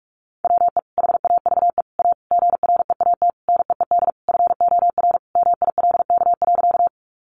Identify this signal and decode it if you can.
Morse code, heard as a keyed tone, about 44 wpm, 700 Hz. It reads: WE HAVE A GREAT NEED FOR MILK.
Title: Sentences from Top 1500 Words 45wpm